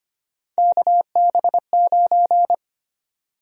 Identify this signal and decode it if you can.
Morse code, heard as a keyed tone, about 25 wpm, 700 Hz. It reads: KB9
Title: USA Call Sign Prefixes - 25wpm